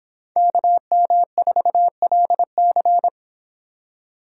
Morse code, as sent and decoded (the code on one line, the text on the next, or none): -.- -- ....- .-.. -.-.
KM4LC